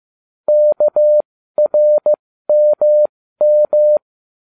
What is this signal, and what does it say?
Morse code, heard as a keyed tone, about 15 wpm, spaced as for 13 wpm, 600 Hz. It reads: KRMM
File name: cw / ubacw07 1